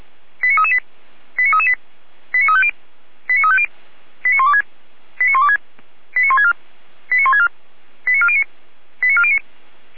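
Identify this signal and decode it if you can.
Five-tone selective call (ZVEI), heard as quick runs of five tones, tones of about 70 ms. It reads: 89338 89338 89350 89350 89126 89126 89164 89164 89409 89409
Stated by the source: ZVEI five-tone call received on an SDR.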